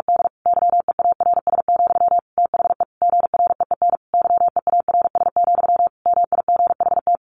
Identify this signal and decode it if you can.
Morse code, heard as a keyed tone, about 46 wpm, 700 Hz. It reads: D YEARS, THE GREEN YEARS, MIGHT
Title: A Sound Of Thunder 45wpm